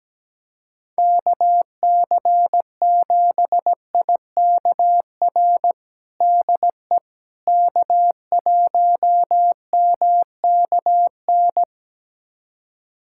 Morse code, heard as a keyed tone, about 17 wpm, 700 Hz. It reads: KC7IKR DE K1MKN
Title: USA Callsign DE Callsign 17wpm